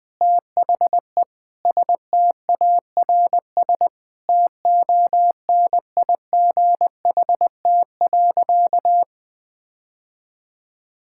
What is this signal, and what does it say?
Morse code, heard as a keyed tone, about 20 wpm, 700 Hz. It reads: THE STARS TONIGHT.
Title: Sentences with 3-Word Phrases - 20wpm